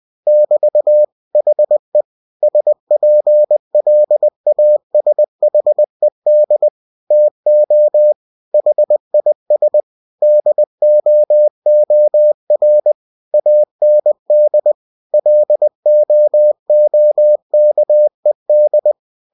Morse code, heard as a keyed tone, about 20 wpm, 600 Hz. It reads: <BT> HE SPLASHED TO HIS DOOR AND LOOKED
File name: cw-books / winnie-the-pooh / winnie-20wpm-page-0094